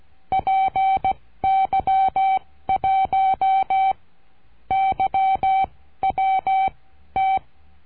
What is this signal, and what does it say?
Morse code, heard as a keyed tone, about 16 wpm, spaced as for 12 wpm, 770 Hz. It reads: PY1YWT
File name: cursos / CURSO DE TELEGRAFIA / Telegrafia - Trilha 22